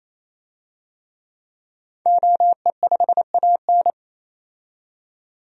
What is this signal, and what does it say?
Morse code, heard as a keyed tone, about 28 wpm, 700 Hz. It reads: OE5AN
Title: DXCC-100 Call Signs - 5 Characters - 28wpm